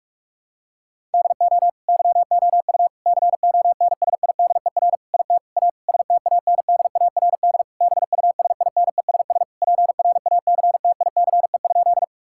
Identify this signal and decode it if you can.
Morse code, heard as a keyed tone, about 47 wpm, 700 Hz. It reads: DO YOU CONSIDER IT A STANDARD BUSINESS PRACTICE?
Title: Sentences from Top 1000 Words - Review Part 6 - 45wpm